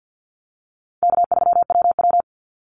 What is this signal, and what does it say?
Morse code, heard as a keyed tone, about 50 wpm, 700 Hz. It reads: K3WW
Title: Call Signs - 50wpm